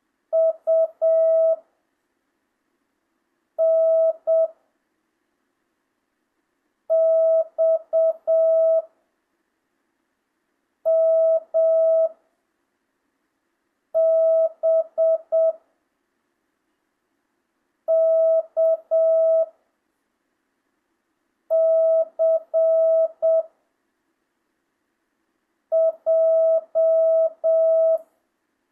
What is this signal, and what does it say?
Morse code, heard as a keyed tone, about 7 wpm, 660 Hz. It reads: U N X M B K C J
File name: jh8cbh / 2025cw